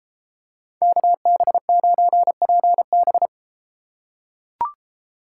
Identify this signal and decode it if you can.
Morse code, heard as a keyed tone, about 32 wpm, 700 Hz. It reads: KB9PB